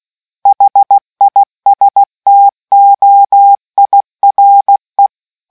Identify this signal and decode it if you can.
Morse code, heard as a keyed tone, about 16 wpm, 800 Hz. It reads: HISTOIRE